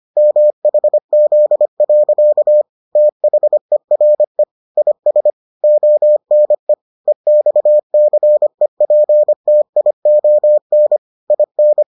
Morse code, heard as a keyed tone, about 25 wpm, 600 Hz. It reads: MHZ. THERE IS ONE EXCEPTION IN